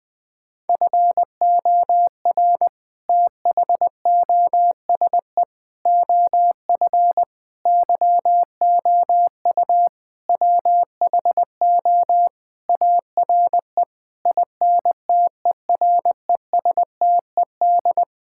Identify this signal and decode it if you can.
Morse code, heard as a keyed tone, about 20 wpm, 700 Hz. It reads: FOR THOSE OF YOU WHO ARE INTERESTED
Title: Sentences with 6-Word Phrases - 20wpm